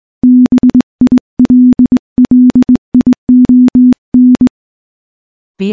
Morse code, heard as a keyed tone, about 20 wpm, 250 Hz. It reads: BILLION
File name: morse / morse-training-20WPM